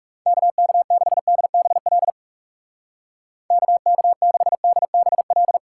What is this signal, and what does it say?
Morse code, heard as a keyed tone, about 36 wpm, 700 Hz. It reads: KK6DBL KK6DBL
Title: Speed Racing - USA 2x3 Call Signs 30wpm